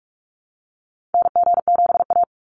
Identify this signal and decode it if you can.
Morse code, heard as a keyed tone, about 44 wpm, 700 Hz. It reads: NG7A
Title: NA SOTA Activator Callsigns 45wpm